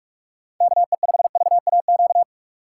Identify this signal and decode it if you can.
Morse code, heard as a keyed tone, about 41 wpm, 700 Hz. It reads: KE5VAQ
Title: USA Call Signs - 6 Characters - 3x Character Spacing - 45wpm